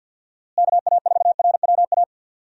Morse code, heard as a keyed tone, about 50 wpm, 700 Hz. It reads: KA4RWA
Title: USA Call Signs - 6 Characters - Repeated 2x - 50wpm